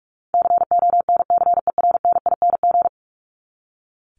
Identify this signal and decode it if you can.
Morse code, heard as a keyed tone, about 44 wpm, 700 Hz. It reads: CONCERNING
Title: Top 1751-2000 Words 45wpm